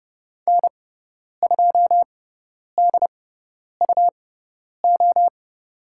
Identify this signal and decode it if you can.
Morse code, heard as a keyed tone, about 30 wpm, spaced as for 10 wpm, 700 Hz. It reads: N2DUO
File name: Lesson 17g - Letter u - Call Signs 30wpm